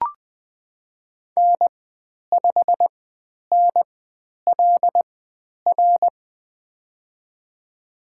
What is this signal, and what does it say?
Morse code, heard as a keyed tone, about 20 wpm, spaced as for 10 wpm, 700 Hz. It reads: N5NLR
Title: 095 Lesson 16g - Letter c - Call Signs 20wpm